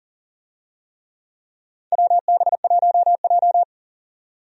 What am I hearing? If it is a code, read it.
Morse code, heard as a keyed tone, about 40 wpm, 700 Hz. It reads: WB1J
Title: USA Call Signs - 4 Characters - 40wpm